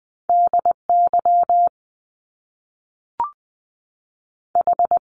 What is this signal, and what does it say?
Morse code, heard as a keyed tone, about 20 wpm, 700 Hz. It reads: D Y H
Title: Top Two Letter Combinations in Words 20wpm